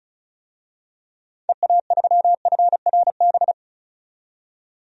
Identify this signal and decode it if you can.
Morse code, heard as a keyed tone, about 34 wpm, 700 Hz. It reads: EA3FRB
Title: DXCC-100 Call Signs - 6 Characters - 35wpm